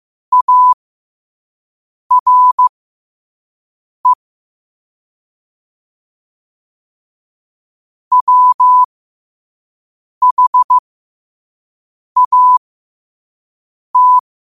Morse code, heard as a keyed tone, about 15 wpm, spaced as for 5 wpm, 1000 Hz. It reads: ARE WHAT
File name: qotdfiles / 5wpm